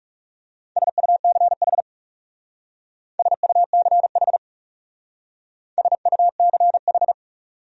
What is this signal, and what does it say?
Morse code, heard as a keyed tone, about 38 wpm, 700 Hz. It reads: SUCH SUCH SUCH